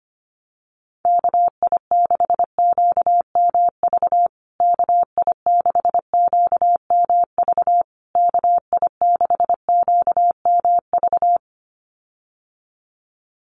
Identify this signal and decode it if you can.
Morse code, heard as a keyed tone, about 25 wpm, 700 Hz. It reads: KI6QMV KI6QMV KI6QMV
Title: USA Call Signs Repeated 3x 25wpm